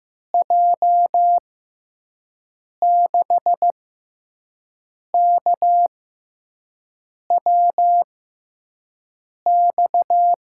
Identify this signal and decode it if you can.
Morse code, heard as a keyed tone, about 15 wpm, spaced as for 5 wpm, 700 Hz. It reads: J6KWX
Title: USA Call Signs - 6 Characters - 6x Character Spacing - 15wpm